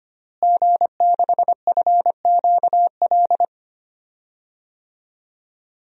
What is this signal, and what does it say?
Morse code, heard as a keyed tone, about 25 wpm, 700 Hz. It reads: G6FQL
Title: DXCC-100 Call Signs - 5 Characters - Repeated 2x - 25wpm